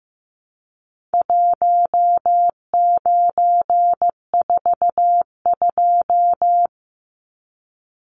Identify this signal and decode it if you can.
Morse code, heard as a keyed tone, about 15 wpm, 700 Hz. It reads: 1942
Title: Numbers - Contest Style - 4-Digit 15wpm